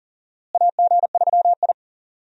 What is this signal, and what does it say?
Morse code, heard as a keyed tone, about 40 wpm, 700 Hz. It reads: AG3I
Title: USA Call Signs - 4 Characters - Repeated 2x - 40wpm